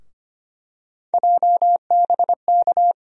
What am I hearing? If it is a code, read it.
Morse code, heard as a keyed tone, about 25 wpm, 700 Hz. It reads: JBK
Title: USA Call Sign Postfixes - 3 Letters 25wpm